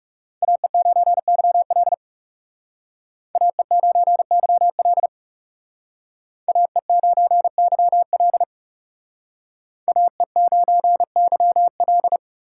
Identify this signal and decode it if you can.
Morse code, heard as a keyed tone, about 38 wpm, 700 Hz. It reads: AE9YL AE9YL AE9YL AE9YL